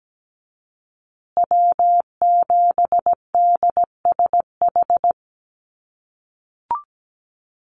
Morse code, heard as a keyed tone, about 17 wpm, 700 Hz. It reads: W7DSH
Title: USA Call Signs Repeated 3x 17wpm